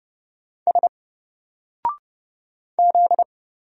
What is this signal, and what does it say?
Morse code, heard as a keyed tone, about 30 wpm, 700 Hz. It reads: S Z
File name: Single-Letter-Number-30wpm